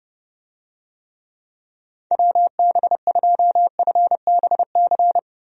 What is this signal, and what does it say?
Morse code, heard as a keyed tone, about 30 wpm, 700 Hz. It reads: WB2FBC
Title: USA Call Signs - 6 Characters - Repeated 3x - 30wpm